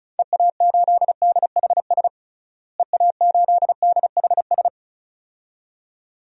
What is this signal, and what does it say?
Morse code, heard as a keyed tone, about 34 wpm, 700 Hz. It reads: EA8DHS EA8DHS
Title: DXCC-100 Call Signs - 6 Characters - Repeated 2x - 35wpm